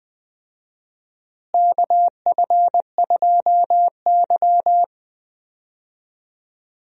Morse code, heard as a keyed tone, about 21 wpm, 700 Hz. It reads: KF2Y
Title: USA Call Signs - 4 Characters - Repeated 2x - 20wpm